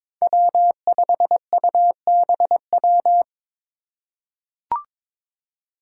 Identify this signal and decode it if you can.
Morse code, heard as a keyed tone, about 22 wpm, 700 Hz. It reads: W5UBW
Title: Five Challenging Letter-Number Combinations 22wpm